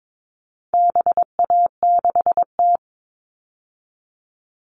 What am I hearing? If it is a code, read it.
Morse code, heard as a keyed tone, about 22 wpm, 700 Hz. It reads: BA6T
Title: All Cut Numbers - 4-Digit 22wpm